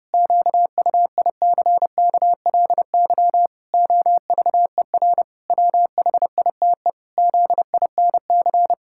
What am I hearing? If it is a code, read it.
Morse code, heard as a keyed tone, about 32 wpm, 700 Hz. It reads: QUICKLY OVER WHITE ZINC
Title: Pangram Challenge v2 - 30wpm